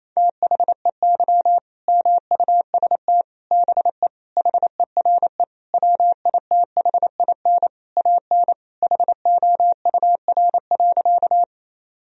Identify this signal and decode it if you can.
Morse code, heard as a keyed tone, about 28 wpm, 700 Hz. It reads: THEY MUST BE HERE WITHIN AN HOUR.
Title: Sentences from Top 1000 Words - Review Part 4 - 28wpm